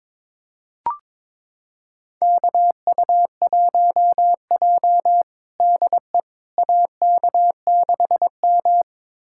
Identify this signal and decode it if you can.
Morse code, heard as a keyed tone, about 22 wpm, 700 Hz. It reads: KU1J DE AK6M
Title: USA Callsign DE Callsign - 4 Characters - 22wpm